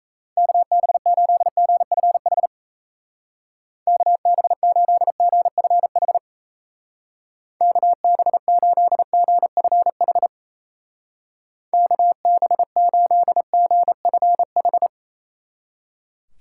Morse code, heard as a keyed tone, about 36 wpm, 700 Hz. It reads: KB8GFH KB8GFH KB8GFH KB8GFH